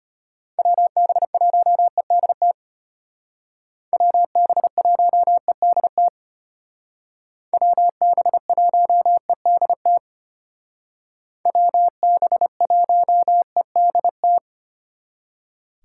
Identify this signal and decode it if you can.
Morse code, heard as a keyed tone, about 32 wpm, 700 Hz. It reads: WB1EDT WB1EDT WB1EDT WB1EDT